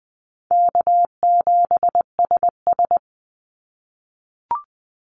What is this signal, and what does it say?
Morse code, heard as a keyed tone, about 20 wpm, 700 Hz. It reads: K7SS
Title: Easy Call Signs - 20wpm